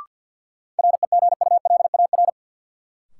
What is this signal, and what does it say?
Morse code, heard as a keyed tone, about 51 wpm, 700 Hz. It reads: REGULAR
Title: Sentences from Top 900 Words - New Word Drill - 50wpm